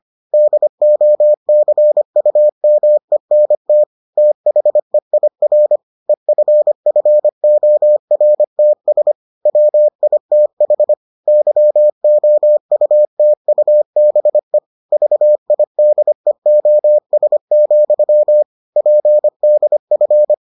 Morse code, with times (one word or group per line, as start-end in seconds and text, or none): 0.3-3.8 DOCUMENT
4.2-5.8 THEIR
6.1-9.1 EFFORTS
9.5-10.9 WITH
11.3-14.6 YOUTUBE
14.9-18.4 VIDEOS,
18.8-20.3 PDF